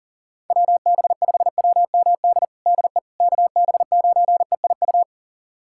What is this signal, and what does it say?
Morse code, heard as a keyed tone, about 40 wpm, 700 Hz. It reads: WB5WMD DE KB9EIU
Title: USA Callsign DE Callsign - 6 Characters - 40wpm